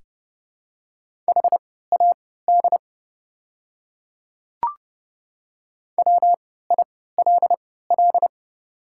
Morse code, heard as a keyed tone, about 30 wpm, spaced as for 17 wpm, 700 Hz. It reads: HAD WILL
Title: 3x Character Spacing - Top 100 Words - 30wpm